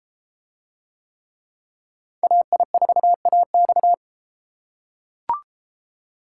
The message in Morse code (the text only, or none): AI4AX